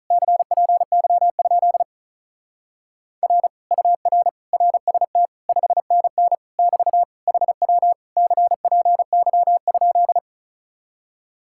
Morse code, heard as a keyed tone, about 42 wpm, 700 Hz. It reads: CPY? R UR RST 5NN <BT> HW CPY?
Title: Speed-Racing Common QSO Phrases Encore - 35wpm